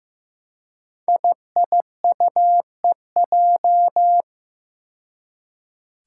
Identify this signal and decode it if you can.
Morse code, heard as a keyed tone, about 15 wpm, 700 Hz. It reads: IIUEJ
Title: Five Challenging Letter-Number Combinations 15wpm